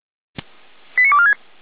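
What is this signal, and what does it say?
Five-tone selective call (ZVEI), heard as quick runs of five tones, tones of about 80 ms. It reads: 89246
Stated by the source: ZVEI five-tone selective call received by SDR.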